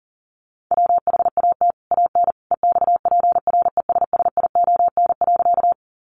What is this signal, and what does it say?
Morse code, heard as a keyed tone, about 40 wpm, 700 Hz. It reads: WHAT AN EXPRESSION.
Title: Sentences from Top 1200 Words 40wpm